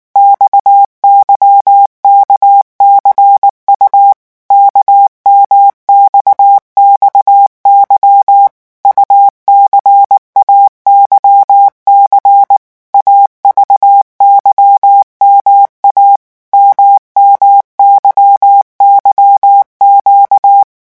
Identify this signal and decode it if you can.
Morse code, heard as a keyed tone, about 19 wpm, 800 Hz. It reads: XYKCU KMXXY UCAYC AVYMA MMYYQ
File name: morse mp3 / revision2-vitesse 18 mots